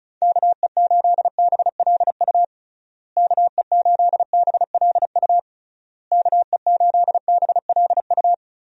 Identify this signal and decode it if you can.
Morse code, heard as a keyed tone, about 36 wpm, 700 Hz. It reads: KE8BLU KE8BLU KE8BLU
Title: USA Call Signs - 6 Characters - Repeated 3x - 35wpm